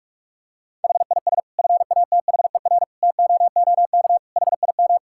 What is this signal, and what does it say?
Morse code, heard as a keyed tone, about 46 wpm, 700 Hz. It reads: HIS FATHER TOOK HIM
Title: Sentences from Top 1000 Words - Review Part 5 - 45wpm